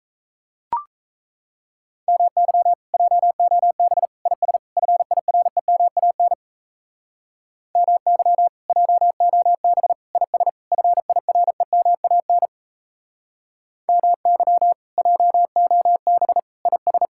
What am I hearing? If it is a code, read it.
Morse code, heard as a keyed tone, about 34 wpm, 700 Hz. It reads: MY JOB IS FIREMAN MY JOB IS FIREMAN MY JOB IS
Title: Speed-Racing Common QSO Phrases - 28wpm